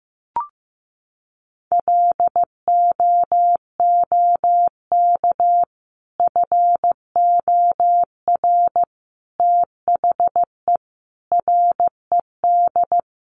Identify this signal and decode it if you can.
Morse code, heard as a keyed tone, about 15 wpm, 700 Hz. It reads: LOOK FOR THE RED
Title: A Sound Of Thunder 15wpm